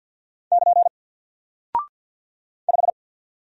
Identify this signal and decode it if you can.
Morse code, heard as a keyed tone, about 50 wpm, 700 Hz. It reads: <KN> 5
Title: Single Character (Letter:Number:Punctuation:Prosign) Alt 50wpm